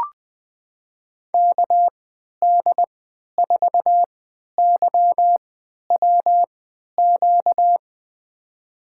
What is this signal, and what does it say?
Morse code, heard as a keyed tone, about 20 wpm, spaced as for 11 wpm, 700 Hz. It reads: KD4YWQ
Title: USA Call Signs - 6 Characters - 3x Character Spacing - 20wpm